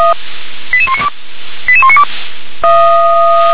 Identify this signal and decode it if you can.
Five-tone selective call (ZVEI), heard as quick runs of five tones, tones of about 70 ms. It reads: 88192 88192